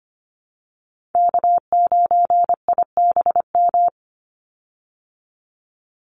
Morse code, heard as a keyed tone, about 25 wpm, 700 Hz. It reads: K9IBM
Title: Call Signs Repeated 2x - 25wpm